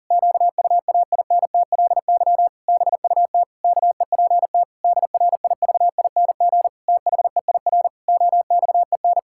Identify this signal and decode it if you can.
Morse code, heard as a keyed tone, about 40 wpm, 700 Hz. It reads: QUAINTLY BUT KEPT DRIVING THEIR OXEN
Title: Pangram Challenge v2 - 40wpm